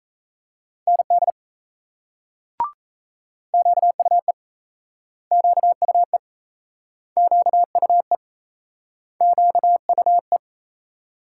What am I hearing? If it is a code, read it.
Morse code, heard as a keyed tone, about 38 wpm, 700 Hz. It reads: ND QUE QUE QUE QUE